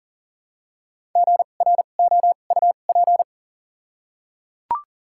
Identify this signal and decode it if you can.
Morse code, heard as a keyed tone, about 39 wpm, spaced as for 29 wpm, 700 Hz. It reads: GROUP
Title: 2x Character Spacing - Top 300 Words - 40wpm